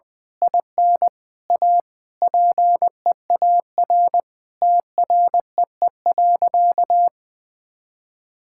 Morse code, heard as a keyed tone, about 20 wpm, 700 Hz. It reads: IN A PEAR TREE.
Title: Sentences with 9-Word Phrases - 20wpm